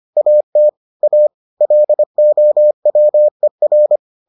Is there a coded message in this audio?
Yes: Morse code from 0.2 to 4.0 s, heard as a keyed tone, about 25 wpm, 600 Hz. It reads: AT A LOWER